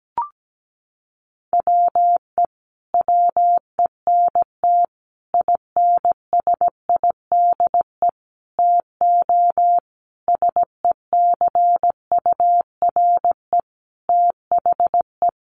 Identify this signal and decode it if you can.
Morse code, heard as a keyed tone, about 17 wpm, 700 Hz. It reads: WE WENT INSIDE TO SECURE THE
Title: Sentences from Top 1200 Words 17wpm